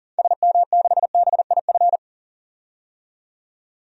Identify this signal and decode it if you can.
Morse code, heard as a keyed tone, about 40 wpm, 700 Hz. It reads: SM6BIF
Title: DXCC-100 Call Signs - 6 Characters - 40wpm